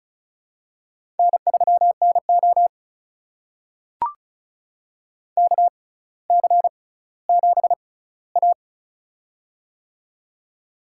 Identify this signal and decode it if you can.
Morse code, heard as a keyed tone, about 36 wpm, spaced as for 18 wpm, 700 Hz. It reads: N3NO KC7A
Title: USA Call Signs - 4 Characters - 6x Character Spacing - 35wpm